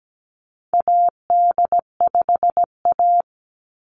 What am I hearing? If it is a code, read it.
Morse code, heard as a keyed tone, about 17 wpm, 700 Hz. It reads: AD5A
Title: Call Signs Repeated 2x - 17wpm